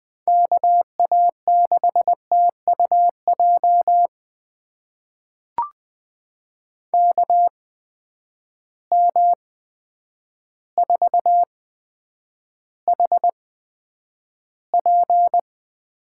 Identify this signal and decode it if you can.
Morse code, heard as a keyed tone, about 20 wpm, spaced as for 9 wpm, 700 Hz. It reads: KA6TUJ KM4HP
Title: USA Call Signs - 6 Characters - 8x Character Spacing - 20wpm